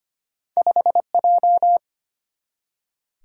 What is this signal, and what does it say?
Morse code, heard as a keyed tone, about 25 wpm, 700 Hz. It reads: 5J
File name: Two Letter-Number Combinations - 25wpm